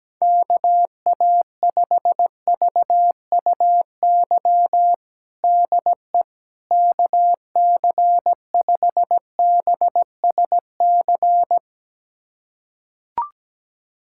Morse code, heard as a keyed tone, about 17 wpm, 700 Hz. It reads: KA5VUY DE KC5BSC
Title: USA Callsign DE Callsign - 6 Characters - 17wpm